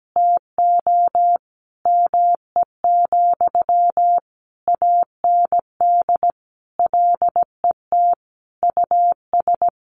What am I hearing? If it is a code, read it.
Morse code, heard as a keyed tone, about 17 wpm, 700 Hz. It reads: TO ME, AND LET US
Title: Sentences from Top 1300 Words 17wpm